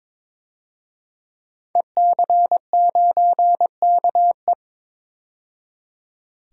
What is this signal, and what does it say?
Morse code, heard as a keyed tone, about 22 wpm, 700 Hz. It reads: EC9KE
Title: DXCC-100 Call Signs - 5 Characters - 22wpm